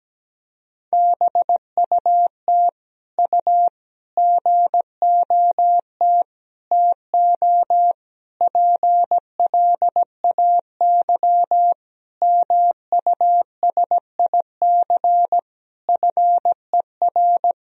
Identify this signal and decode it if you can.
Morse code, heard as a keyed tone, about 17 wpm, 700 Hz. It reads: BUT U GOT TO PLAY MUSIC FER
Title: Sentences from Top 300 Words and Abbreviations 17wpm